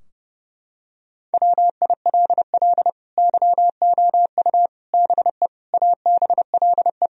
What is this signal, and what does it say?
Morse code, heard as a keyed tone, about 32 wpm, 700 Hz. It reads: WILL YOU BE ABLE
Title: Sentences from Top 1000 Words - Review Part 2 - 30wpm